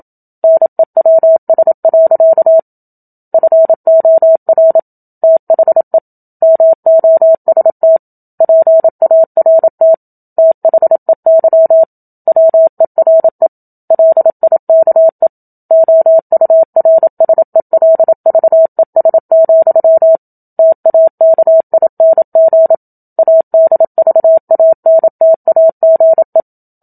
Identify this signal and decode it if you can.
Morse code, heard as a keyed tone, about 27 wpm, 650 Hz. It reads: NEWS. FOR THE MOST PART THEY WERE LIKE OURSELVES, TAKING ADVANTAGE
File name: Book Two / 01-Chapter 1 - 27 WPM